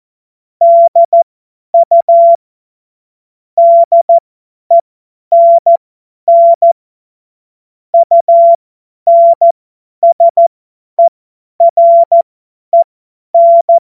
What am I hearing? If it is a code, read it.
Morse code, heard as a keyed tone, about 14 wpm, spaced as for 10 wpm, 680 Hz. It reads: DU DENN UNSEREN
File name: Morsekurs / Witze / Witz 07. März 2023 50 F70